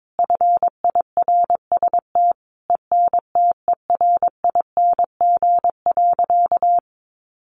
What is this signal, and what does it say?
Morse code, heard as a keyed tone, about 22 wpm, 700 Hz. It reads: FIRST ENTERING.
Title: A Sound Of Thunder 22wpm